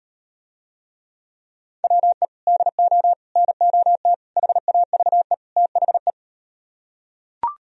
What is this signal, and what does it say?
Morse code, heard as a keyed tone, about 38 wpm, 700 Hz. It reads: WE DO NOT HAVE THE